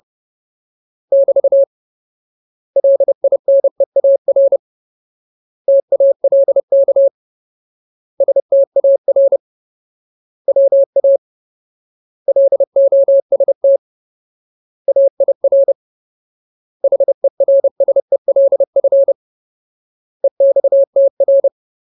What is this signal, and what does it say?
Morse code, heard as a keyed tone, about 30 wpm, 550 Hz. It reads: <BT> LINEAR TALK STAR WA LOST AIR HERSELF EXTR
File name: Ultimate Word List / 30WPM Ultimate